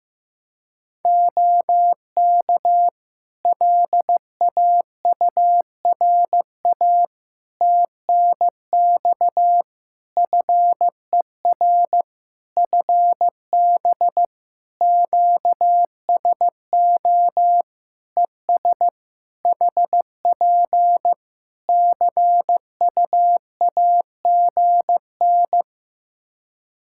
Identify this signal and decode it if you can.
Morse code, heard as a keyed tone, about 15 wpm, 700 Hz. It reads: OK LAURA TNX FER FB QSO ES HP CUAGN
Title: K1USN SST QSO Practice 15wpm